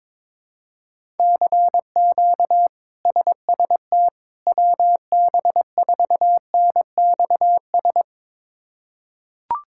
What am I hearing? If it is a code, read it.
Morse code, heard as a keyed tone, about 22 wpm, 700 Hz. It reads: CQ SST WB4NXS
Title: K1USN SST QSO Practice - 22wpm